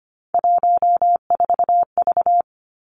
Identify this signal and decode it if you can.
Morse code, heard as a keyed tone, about 25 wpm, 700 Hz. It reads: 14V
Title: Three Challenging Letter-Number Combinations 25wpm